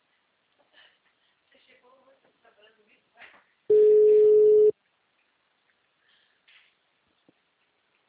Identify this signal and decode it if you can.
Ringback tone (Europe, 425 Hz), with speech.